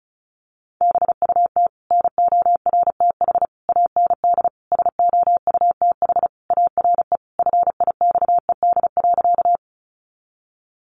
Morse code, heard as a keyed tone, about 34 wpm, 700 Hz. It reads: BUT NORTH AND SOUTH ARE FIXED.
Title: Sentences from Top 1100 Words 35wpm